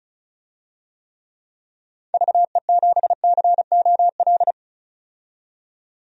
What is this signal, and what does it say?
Morse code, heard as a keyed tone, about 36 wpm, 700 Hz. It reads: VE7COL
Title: DXCC-100 Call Signs - 6 Characters - 35wpm